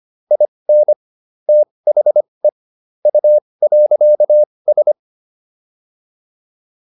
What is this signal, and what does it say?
Morse code, heard as a keyed tone, about 25 wpm, spaced as for 20 wpm, 600 Hz. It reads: IN THE U.S